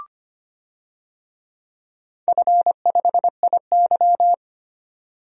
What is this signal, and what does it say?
Morse code, heard as a keyed tone, about 25 wpm, 700 Hz. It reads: F5IY
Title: DXCC-100 Call Signs - 4 Characters - 25wpm